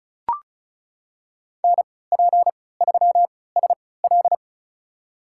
Morse code, heard as a keyed tone, about 34 wpm, spaced as for 20 wpm, 700 Hz. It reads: NP3SL
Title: USA Call Signs - 5 Characters - 3x Character Spacing - 35wpm